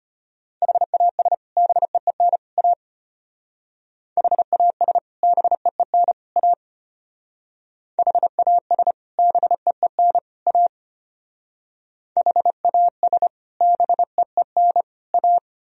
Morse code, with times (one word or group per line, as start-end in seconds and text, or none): 0.6-1.3 HAS
1.6-2.4 BEEN
2.6-2.7 A
4.2-5.0 HAS
5.2-6.1 BEEN
6.4-6.5 A
8.0-8.9 HAS
9.2-10.2 BEEN
10.5-10.7 A
12.2-13.3 HAS
13.6-14.8 BEEN
15.1-15.4 A